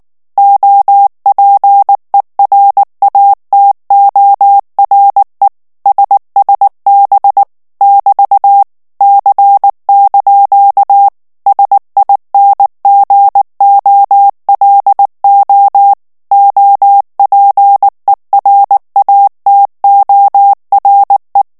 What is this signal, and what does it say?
Morse code, heard as a keyed tone, about 19 wpm, 800 Hz. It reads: OPERATORE SSB <BT> C) SINGOLO OPERATORE
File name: morse mp3 / ITALIEN texte clair 2 vitesse 18 mots minute